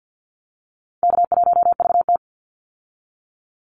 Morse code, heard as a keyed tone, about 50 wpm, 700 Hz. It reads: K1VT
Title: Call Signs - 50wpm